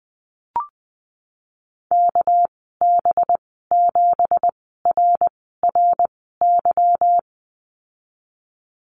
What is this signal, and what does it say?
Morse code, heard as a keyed tone, about 20 wpm, spaced as for 15 wpm, 700 Hz. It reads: KB7RRY
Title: 200 Lesson 32g - Letter k - Call Signs 20wpm